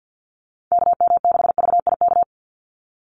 Morse code, heard as a keyed tone, about 52 wpm, 700 Hz. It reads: KM6VIK
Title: USA Call Signs 50wpm